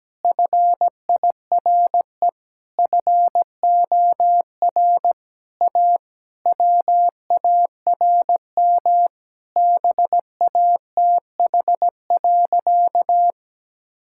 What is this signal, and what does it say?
Morse code, heard as a keyed tone, about 17 wpm, 700 Hz. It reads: FIRE FOR A WARM BATH.
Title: Sentences from Top 1600 Words 17wpm